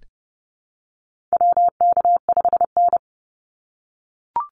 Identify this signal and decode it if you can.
Morse code, heard as a keyed tone, about 30 wpm, 700 Hz. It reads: WK5N
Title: Call Signs - 30wpm